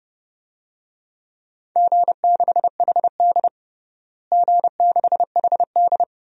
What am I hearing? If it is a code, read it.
Morse code, heard as a keyed tone, about 30 wpm, 700 Hz. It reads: G6HD G6HD
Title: DXCC-100 Call Signs - 4 Characters - Repeated 2x - 30wpm